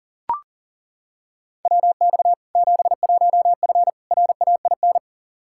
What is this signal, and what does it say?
Morse code, heard as a keyed tone, about 40 wpm, 700 Hz. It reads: WX 71F RAIN
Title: K1USN SST QSO Practice 40wpm